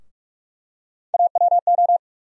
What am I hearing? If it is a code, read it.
Morse code, heard as a keyed tone, about 47 wpm, 700 Hz. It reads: AWO
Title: USA Call Sign Postfixes - 3 Letters 45wpm